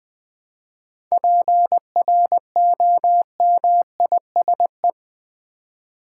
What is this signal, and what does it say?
Morse code, heard as a keyed tone, about 20 wpm, 700 Hz. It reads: PROMISE
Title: 7 Letter Words - First 3 Letters Given - 20wpm